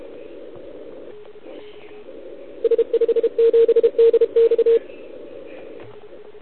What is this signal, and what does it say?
Morse code, heard as a keyed tone, about 31 wpm, 440 Hz. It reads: S57DX
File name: ARCHIV / OKDX2020 / OL4Wa